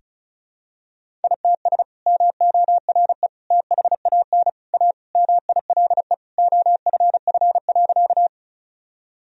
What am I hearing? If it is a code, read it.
Morse code, heard as a keyed tone, about 34 wpm, 700 Hz. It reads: ITS MORE THAN A MILE OFF.
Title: Sentences from Top 1000 Words - Review Part 1 - 35wpm